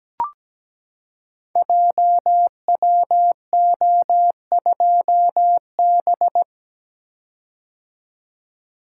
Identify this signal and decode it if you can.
Morse code, heard as a keyed tone, about 17 wpm, 700 Hz. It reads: JWO2B
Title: Code Groups - 5 Letter:Numbers - 17wpm